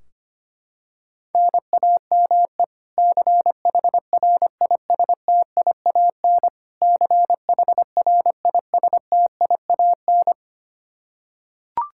Morse code, heard as a keyed tone, about 25 wpm, 700 Hz. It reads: NAME CHRISTIAN CHRISTIAN
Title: Practice QSO - Exchange 1 - Beginning - 25wpm